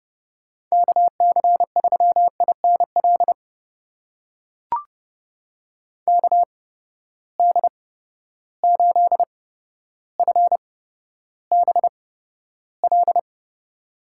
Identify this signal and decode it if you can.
Morse code, heard as a keyed tone, about 32 wpm, spaced as for 13 wpm, 700 Hz. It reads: KC3INL KD8FBL
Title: USA Call Signs - 6 Characters - 8x Character Spacing - 30wpm